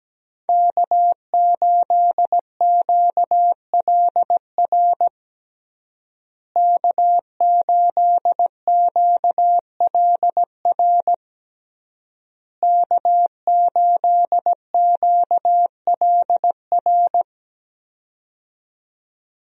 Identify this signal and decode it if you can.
Morse code, heard as a keyed tone, about 17 wpm, 700 Hz. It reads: K8QLR K8QLR K8QLR
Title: USA Call Signs - 5 Characters - Repeated 3x - 17wpm